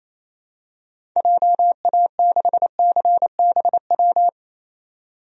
Morse code, heard as a keyed tone, about 27 wpm, 700 Hz. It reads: JA6CBW
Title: DXCC-100 Call Signs - 6 Characters - 28wpm